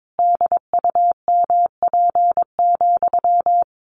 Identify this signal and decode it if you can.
Morse code, heard as a keyed tone, about 22 wpm, 700 Hz. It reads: DUMP,
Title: A Sound Of Thunder 22wpm